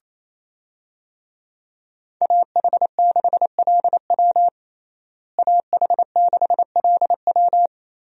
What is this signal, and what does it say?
Morse code, heard as a keyed tone, about 27 wpm, 700 Hz. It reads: AH6LW AH6LW
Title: USA Call Signs - 5 Characters - Repeated 2x - 28wpm